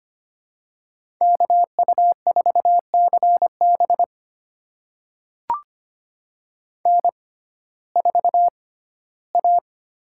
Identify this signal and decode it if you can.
Morse code, heard as a keyed tone, about 25 wpm, spaced as for 15 wpm, 700 Hz. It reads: KU4CB N4A
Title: USA Call Signs - 5 Characters - 6x Character Spacing - 25wpm